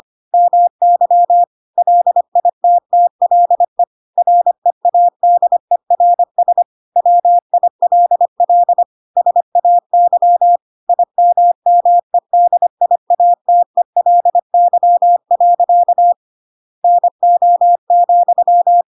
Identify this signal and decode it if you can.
Morse code, heard as a keyed tone, about 25 wpm, 700 Hz. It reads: MY LITTLE READERS WILL SAY IMMEDIATELY. NO,
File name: Pinocchio / Pinocchio ch1 25 25 ik1hge